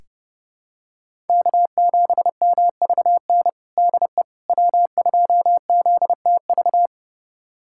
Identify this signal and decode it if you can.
Morse code, heard as a keyed tone, about 30 wpm, 700 Hz. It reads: K7MVN DE W2ZTV
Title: USA Callsign DE Callsign 30wpm